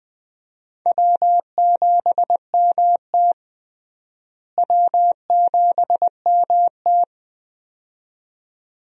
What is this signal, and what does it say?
Morse code, heard as a keyed tone, about 20 wpm, 700 Hz. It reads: W7MT W7MT
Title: USA Call Signs - 4 Characters - Repeated 2x - 20wpm